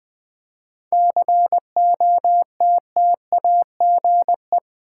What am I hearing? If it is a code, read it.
Morse code, heard as a keyed tone, about 20 wpm, 700 Hz. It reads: COTTAGE
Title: 7 Letter Words - First 3 Letters Given - 20wpm